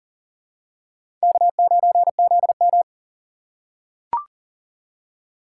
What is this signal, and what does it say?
Morse code, heard as a keyed tone, about 40 wpm, 700 Hz. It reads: K9ZM
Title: USA Call Signs - 4x Character Spacing - 40wpm